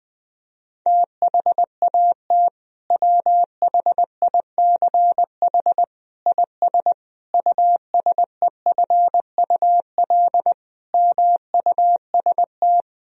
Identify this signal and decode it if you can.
Morse code, heard as a keyed tone, about 21 wpm, 700 Hz. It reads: THAT WHICH IS USEFUL MUST
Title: Sentences from Top 1000 Words - Review Part 4 - 20wpm